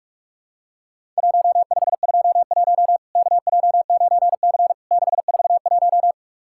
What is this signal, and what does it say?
Morse code, heard as a keyed tone, about 44 wpm, 700 Hz. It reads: 1521 KJ9C 641
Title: Worked All Europe DX Contest - QTC Exchange - 45wpm